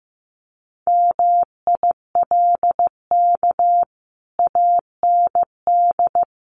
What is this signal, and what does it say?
Morse code, heard as a keyed tone, about 15 wpm, 700 Hz. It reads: MILK AND
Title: Sentences from Top 1500 Words 15wpm